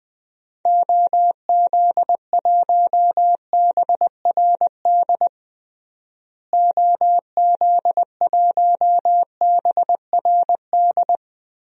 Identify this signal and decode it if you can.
Morse code, heard as a keyed tone, about 21 wpm, 700 Hz. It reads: OZ1BRD OZ1BRD
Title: DXCC-100 Call Signs - 6 Characters - Repeated 3x - 20wpm